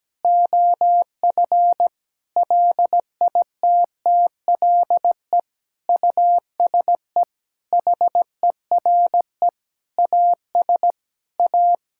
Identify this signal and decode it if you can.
Morse code, heard as a keyed tone, about 17 wpm, 700 Hz. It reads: OF LITTLE USE HERE AS A
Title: Sentences from Top 1000 Words - Review Part 5 - 17wpm